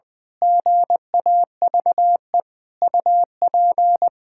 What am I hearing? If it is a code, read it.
Morse code, heard as a keyed tone, about 21 wpm, 700 Hz. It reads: GAVE UP
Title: Pangram Challenge v2 - 20wpm